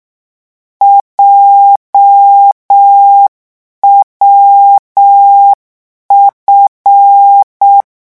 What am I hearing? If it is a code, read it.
Morse code, heard as a keyed tone, about 6 wpm, 800 Hz. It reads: JWF
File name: morse mp3 / lecon7-vitesse 6